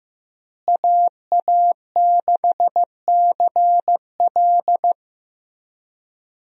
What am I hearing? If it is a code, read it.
Morse code, heard as a keyed tone, about 15 wpm, 700 Hz. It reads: AA6CL
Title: USA Call Signs - 5 Characters - Repeated 2x - 15wpm